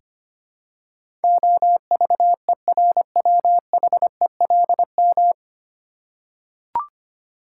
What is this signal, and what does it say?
Morse code, heard as a keyed tone, about 25 wpm, 700 Hz. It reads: OVERWHELM
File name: Nine Letter Words - 25wpm